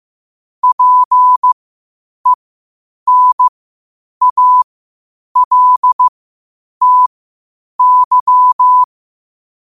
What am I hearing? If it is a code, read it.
Morse code, heard as a keyed tone, about 15 wpm, spaced as for 8 wpm, 1000 Hz. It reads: PENALTY
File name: qotdfiles / 8wpm